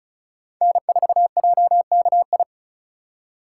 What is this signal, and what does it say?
Morse code, heard as a keyed tone, about 36 wpm, 700 Hz. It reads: N4JKI
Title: USA Call Signs - 5 Characters - 2x Character Spacing - 35wpm